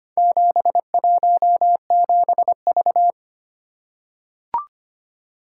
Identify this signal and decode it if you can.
Morse code, heard as a keyed tone, about 25 wpm, 690 Hz. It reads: 717V
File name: practice / 25wpm / Four Letter-Number Combinations - 25wpm